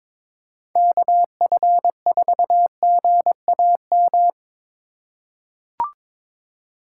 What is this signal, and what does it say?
Morse code, heard as a keyed tone, about 22 wpm, 700 Hz. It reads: KF4GAM
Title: USA Call Signs - 4x Character Spacing - 22wpm